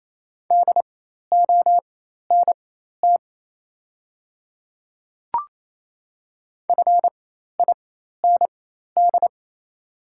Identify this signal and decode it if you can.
Morse code, heard as a keyed tone, about 27 wpm, spaced as for 13 wpm, 700 Hz. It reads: DONT FIND
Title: 4x Character Spacing - Top 300 Words - 28wpm